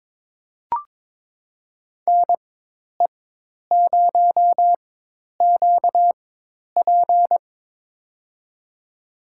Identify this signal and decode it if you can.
Morse code, heard as a keyed tone, about 22 wpm, spaced as for 10 wpm, 700 Hz. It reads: NE0QP
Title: USA Call Signs - 5 Characters - 4x Character Spacing - 22wpm